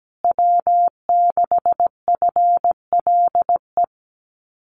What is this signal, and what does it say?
Morse code, heard as a keyed tone, about 17 wpm, 700 Hz. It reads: W6FLE
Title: Common QSO Phrases Encore 17wpm